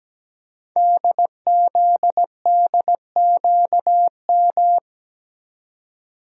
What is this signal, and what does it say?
Morse code, heard as a keyed tone, about 17 wpm, 700 Hz. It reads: DZDQM
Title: Code Groups - 5 Letters - 17wpm